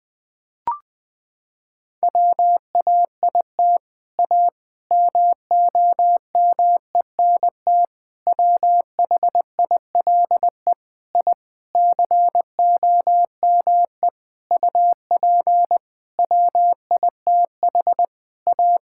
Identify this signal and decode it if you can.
Morse code, heard as a keyed tone, about 20 wpm, 700 Hz. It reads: WAIT A MOMENT WHILE I COME UP WITH A
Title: Sentences from Top 1000 Words - Review Part 5 - 20wpm